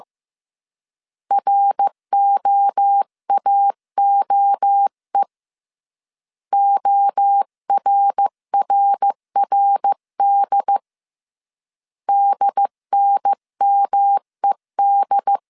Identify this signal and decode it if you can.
Morse code, heard as a keyed tone, about 15 wpm, 800 Hz. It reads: ROAOE ORRRD DNMED
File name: CW / DET / 1 3 1500C S A 14wpm 2447